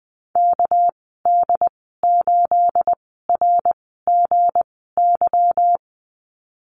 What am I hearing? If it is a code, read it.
Morse code, heard as a keyed tone, about 20 wpm, spaced as for 14 wpm, 700 Hz. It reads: KD8RGY
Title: 231 Lesson 37g - Letter x - Call Signs 20wpm